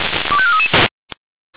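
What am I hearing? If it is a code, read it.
Five-tone selective call (ZVEI), heard as quick runs of five tones, tones of about 70 ms. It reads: 25433
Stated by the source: ZVEI five-tone selective call received via SDR, code 25417, feuerwehr-orlamuende fire brigade.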